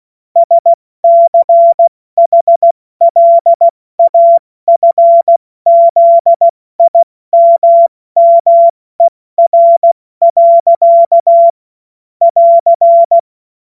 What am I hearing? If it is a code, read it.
Morse code, heard as a keyed tone, about 16 wpm, 680 Hz. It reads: SCHLAFZIMMER. <AR>
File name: Witz-02-September-2025-70-F80